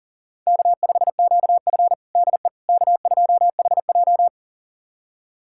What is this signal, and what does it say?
Morse code, heard as a keyed tone, about 40 wpm, 700 Hz. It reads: K5QF DE K2HJ
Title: USA Callsign DE Callsign - 4 Characters - 40wpm